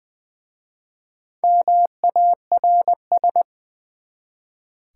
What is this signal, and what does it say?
Morse code, heard as a keyed tone, about 20 wpm, 700 Hz. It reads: MARS
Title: Easy Words with Hints - 20wpm